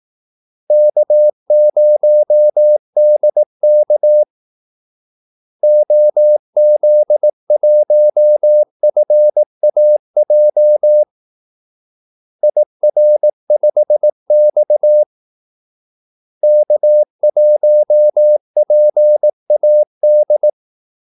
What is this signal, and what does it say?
Morse code, heard as a keyed tone, about 18 wpm, 600 Hz. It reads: K0DK OZ1FAJ IR5X K1PAD
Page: Kaldesignaler 18 wpm